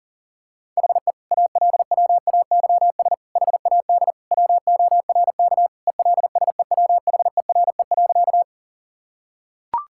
Morse code, heard as a keyed tone, about 40 wpm, 700 Hz. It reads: HE ALWAYS HAD WORK ELSEWHERE.
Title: Sentences from Top 1600 Words 40wpm